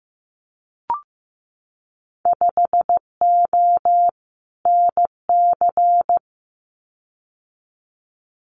Morse code, heard as a keyed tone, about 15 wpm, 700 Hz. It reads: 5O NC
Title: Winter Field Day Exchanges 15wpm